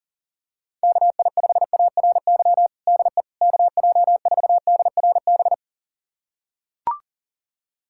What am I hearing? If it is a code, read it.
Morse code, heard as a keyed tone, about 40 wpm, 700 Hz. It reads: KI5ARY DE KJ4DRB
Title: USA Callsign DE Callsign 40wpm